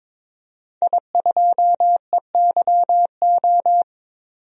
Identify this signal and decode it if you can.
Morse code, heard as a keyed tone, about 22 wpm, 700 Hz. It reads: I2EYO
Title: DXCC-100 Call Signs - 5 Characters - 22wpm